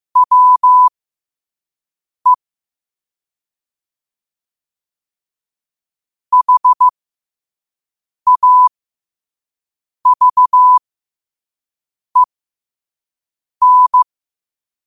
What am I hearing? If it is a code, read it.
Morse code, heard as a keyed tone, about 15 wpm, spaced as for 5 wpm, 1000 Hz. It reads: WE HAVEN
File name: 5wpm